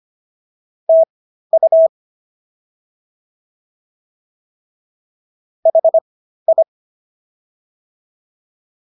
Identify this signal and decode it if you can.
Morse code, heard as a keyed tone, about 25 wpm, spaced as for 13 wpm, 650 Hz. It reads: TU HI